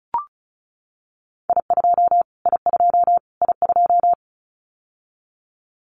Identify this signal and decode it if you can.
Morse code, heard as a keyed tone, about 34 wpm, 700 Hz. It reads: I2 I2 I2
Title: Callsign Prefixes for Top 25 DXCC - Repeated 3x - 35wpm